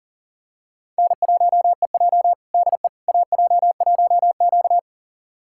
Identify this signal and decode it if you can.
Morse code, heard as a keyed tone, about 40 wpm, 700 Hz. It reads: N1EJ DE AJ1Q
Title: USA Callsign DE Callsign - 4 Characters - 40wpm